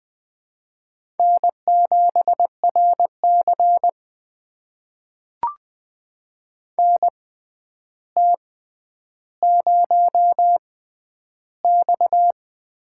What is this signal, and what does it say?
Morse code, heard as a keyed tone, about 20 wpm, spaced as for 10 wpm, 700 Hz. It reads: N7RC NT0X
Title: USA Call Signs - 4 Characters - 6x Character Spacing - 20wpm